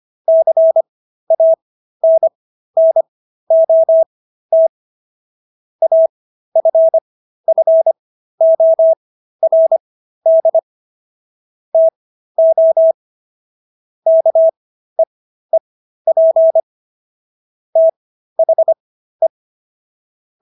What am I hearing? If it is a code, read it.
Morse code, heard as a keyed tone, about 25 wpm, spaced as for 13 wpm, 650 Hz. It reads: CANNOT AFFORD TO KEEP THE